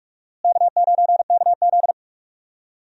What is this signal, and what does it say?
Morse code, heard as a keyed tone, about 46 wpm, 700 Hz. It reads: K9KZ
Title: USA Call Signs - 4 Characters - Repeated 3x - 45wpm